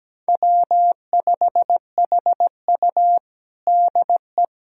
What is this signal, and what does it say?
Morse code, heard as a keyed tone, about 17 wpm, 700 Hz. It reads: W5HU DE
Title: USA Callsign DE Callsign - 4 Characters - 17wpm